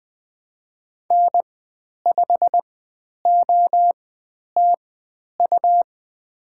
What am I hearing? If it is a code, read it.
Morse code, heard as a keyed tone, about 20 wpm, spaced as for 10 wpm, 700 Hz. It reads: N5OTU
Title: 102 Lesson 17g - Letter u - Call Signs 20wpm